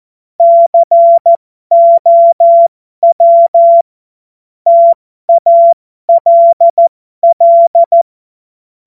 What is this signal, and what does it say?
Morse code, heard as a keyed tone, about 14 wpm, spaced as for 12 wpm, 680 Hz. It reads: COW TALL
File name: Morsekurs / Witze